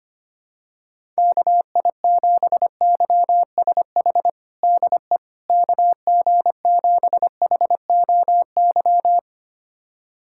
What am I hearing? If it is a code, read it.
Morse code, heard as a keyed tone, about 25 wpm, 700 Hz. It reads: KI7YSH DE KG7HOY
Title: USA Callsign DE Callsign 25wpm